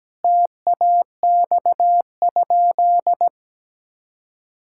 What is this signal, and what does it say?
Morse code, heard as a keyed tone, about 17 wpm, 700 Hz. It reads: TAX?
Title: Sentences from Top 1000 Words - Review Part 5 - 17wpm